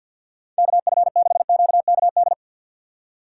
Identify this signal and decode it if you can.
Morse code, heard as a keyed tone, about 53 wpm, 700 Hz. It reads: KF6QKD
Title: USA Call Signs - 6 Characters - 4x Character Spacing - 50wpm